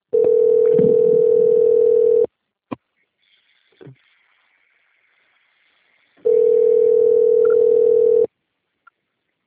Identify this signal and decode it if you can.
Ringback tone (North America), 440 + 480 Hz.